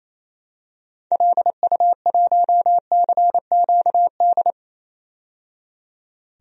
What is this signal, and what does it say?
Morse code, heard as a keyed tone, about 27 wpm, 700 Hz. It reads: LU1CQD
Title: DXCC-100 Call Signs - 6 Characters - 28wpm